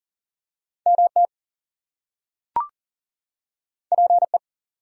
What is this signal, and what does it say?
Morse code, heard as a keyed tone, about 40 wpm, 700 Hz. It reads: MT PE
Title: US States and Canadian Provinces 40wpm